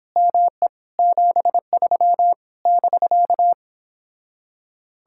Morse code, heard as a keyed tone, about 29 wpm, 700 Hz. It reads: ME 73 <BK>